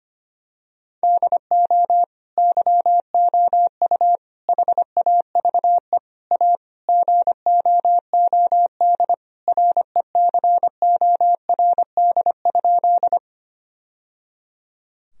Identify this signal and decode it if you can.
Morse code, heard as a keyed tone, about 25 wpm, 700 Hz. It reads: DO YOU HAVE A GOOD RECORD?
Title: Sentences from Top 1000 Words - Review Part 3 - 25wpm